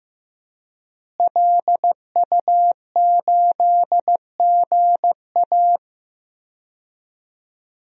Morse code, heard as a keyed tone, about 16 wpm, 700 Hz. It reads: LU8GA
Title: DXCC-100 Call Signs - 5 Characters - Repeated 2x - 15wpm